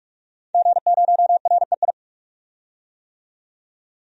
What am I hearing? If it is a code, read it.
Morse code, heard as a keyed tone, about 46 wpm, 700 Hz. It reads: G0REI
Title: DXCC-100 Call Signs - 5 Characters - 45wpm